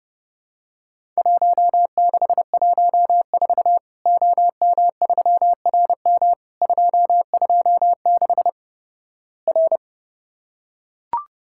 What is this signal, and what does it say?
Morse code, heard as a keyed tone, about 30 wpm, 700 Hz. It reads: 1614 OM3RM 226 R
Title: Worked All Europe DX Contest - QTC Exchange - 30wpm